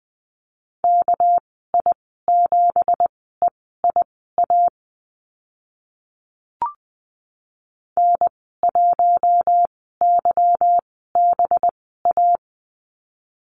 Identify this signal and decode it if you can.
Morse code, heard as a keyed tone, about 20 wpm, spaced as for 14 wpm, 700 Hz. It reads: KI7EIA N1YBA
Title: 207 Lesson 33g - Letter j - Call Signs 20wpm